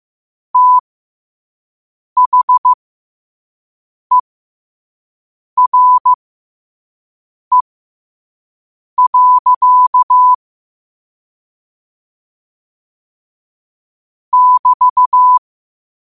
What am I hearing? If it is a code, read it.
Morse code, heard as a keyed tone, about 15 wpm, spaced as for 5 wpm, 1000 Hz. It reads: THERE. <BT>
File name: qotdfiles / 5wpm